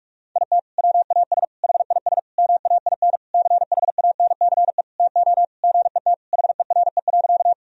Morse code, heard as a keyed tone, about 46 wpm, 700 Hz. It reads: IT WAS HIS MAIN CHANCE TO GET HERE.
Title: Sentences from Top 1000 Words - Review Part 2 - 45wpm